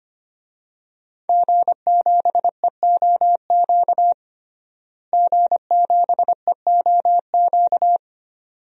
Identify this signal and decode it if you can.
Morse code, heard as a keyed tone, about 25 wpm, 700 Hz. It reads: G7EOQ G7EOQ
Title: DXCC-100 Call Signs - 5 Characters - Repeated 2x - 25wpm